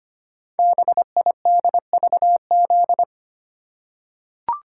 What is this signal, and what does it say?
Morse code, heard as a keyed tone, about 25 wpm, 700 Hz. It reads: BIDVZ
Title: Code Groups - 5 Letter:Numbers - 25wpm